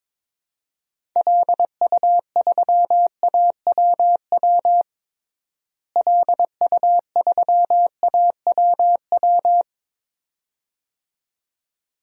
Morse code, heard as a keyed tone, about 22 wpm, 700 Hz. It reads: LU3AWW LU3AWW
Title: DXCC-100 Call Signs - 6 Characters - Repeated 2x - 22wpm